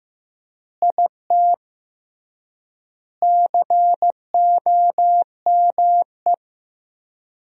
Morse code, heard as a keyed tone, about 15 wpm, 700 Hz. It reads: IT COME
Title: 3x Word-Spacing - Sentences from Top 100 Words - 15wpm